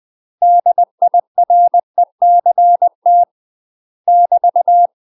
Morse code, heard as a keyed tone, about 20 wpm, 700 Hz. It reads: DIRECT <BT>
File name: UK MASTER QSOs TRAINING / 20WPM / UK MASTER QSOs 0030